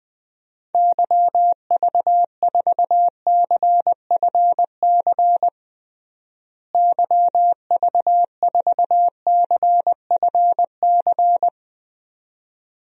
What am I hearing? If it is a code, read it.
Morse code, heard as a keyed tone, about 20 wpm, 700 Hz. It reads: YV4CFC YV4CFC
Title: DXCC-100 Call Signs - 6 Characters - Repeated 2x - 20wpm